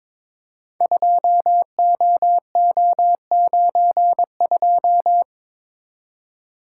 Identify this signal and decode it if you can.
Morse code, heard as a keyed tone, about 23 wpm, 700 Hz. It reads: 2OO92
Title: Code Groups - 5 Letter:Numbers - 22wpm